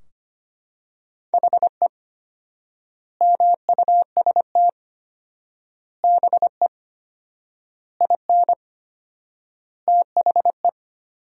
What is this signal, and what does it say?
Morse code, heard as a keyed tone, about 25 wpm, 700 Hz. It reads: HE MUST BE IN THE
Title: 4x Word-Spacing - Sentences from Top 500 Words - 25wpm